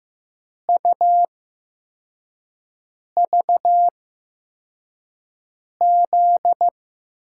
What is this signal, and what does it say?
Morse code, heard as a keyed tone, about 15 wpm, spaced as for 4 wpm, 700 Hz. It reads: UVZ
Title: USA Call Signs - 5 Characters - 8x Character Spacing - 15wpm